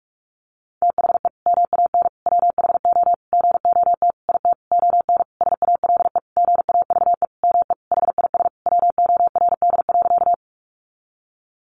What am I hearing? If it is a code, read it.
Morse code, heard as a keyed tone, about 46 wpm, 700 Hz. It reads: THE MAN WHO GOT IT ON SALE GAVE ME HIS WORD.
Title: Sentences from Top 1400 Words 45wpm